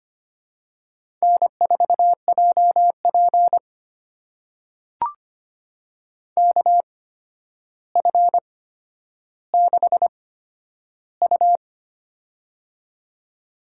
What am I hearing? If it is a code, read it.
Morse code, heard as a keyed tone, about 25 wpm, spaced as for 11 wpm, 700 Hz. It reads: N4JP KF6U
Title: USA Call Signs - 4 Characters - 8x Character Spacing - 25wpm